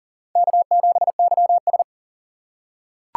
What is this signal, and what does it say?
Morse code, heard as a keyed tone, about 40 wpm, 700 Hz. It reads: K7YS
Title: USA Call Signs - 2x Character Spacing - 40wpm